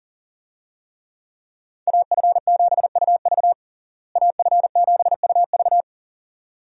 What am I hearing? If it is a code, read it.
Morse code, heard as a keyed tone, about 40 wpm, 700 Hz. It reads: AF7UV AF7UV
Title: USA Call Signs - 5 Characters - Repeated 2x - 40wpm